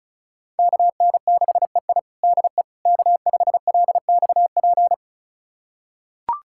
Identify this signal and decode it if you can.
Morse code, heard as a keyed tone, about 36 wpm, 700 Hz. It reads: KN6EI DE K5LXP
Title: USA Callsign DE Callsign - 5 Characters - 35wpm